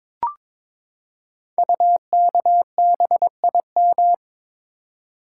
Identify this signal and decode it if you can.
Morse code, heard as a keyed tone, about 22 wpm, 700 Hz. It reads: UKBIM
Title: Code Groups - 5 Letters - 22wpm